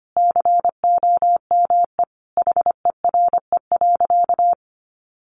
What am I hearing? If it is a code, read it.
Morse code, heard as a keyed tone, about 25 wpm, 700 Hz. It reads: COME HERE.
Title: Sentences from Top 1500 Words 25wpm